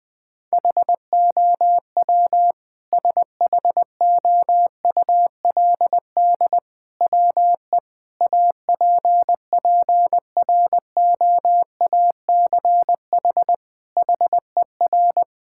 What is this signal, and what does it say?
Morse code, heard as a keyed tone, about 19 wpm, 700 Hz. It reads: HOW SHOULD WE APPROACH HER
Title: Sentences from Top 1000 Words - Review Part 6 - 20wpm